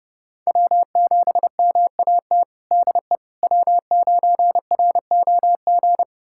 Morse code, heard as a keyed tone, about 30 wpm, 700 Hz. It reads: W7MAT DE W9ROG
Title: USA Callsign DE Callsign - 5 Characters - 30wpm